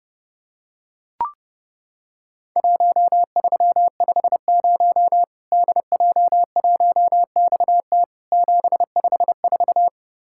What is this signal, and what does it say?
Morse code, heard as a keyed tone, about 30 wpm, 700 Hz. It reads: 1350 DJ1XT 754
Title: Worked All Europe DX Contest - QTC Exchange - 30wpm